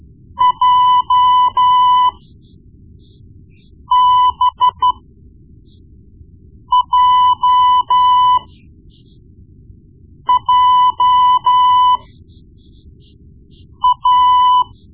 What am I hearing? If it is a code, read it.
Morse code, heard as a keyed tone, about 12 wpm, spaced as for 4 wpm, 1000 Hz. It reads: JBJJA
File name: Morse Lessons / Morse Lessons / Tape-1B